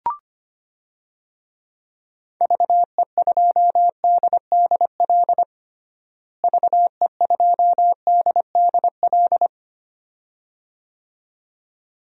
Morse code, heard as a keyed tone, about 25 wpm, 700 Hz. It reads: VE2DDL VE2DDL
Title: DXCC-100 Call Signs - 6 Characters - Repeated 2x - 25wpm